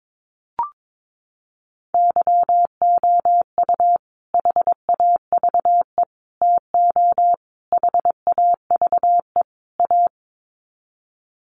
Morse code, heard as a keyed tone, about 22 wpm, 700 Hz. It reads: YOU HAVE TO HAVE A
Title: Sets of 5 Words Encore 22wpm